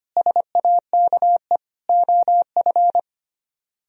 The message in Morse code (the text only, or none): SAKE OF